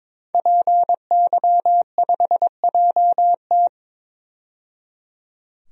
Morse code, heard as a keyed tone, about 22 wpm, 700 Hz. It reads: PY5JT
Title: DXCC-100 Call Signs - 5 Characters - 22wpm